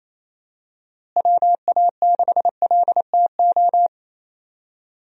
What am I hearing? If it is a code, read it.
Morse code, heard as a keyed tone, about 27 wpm, 700 Hz. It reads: WA6LTO
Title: USA Call Signs - 6 Characters - Repeated 2x - 28wpm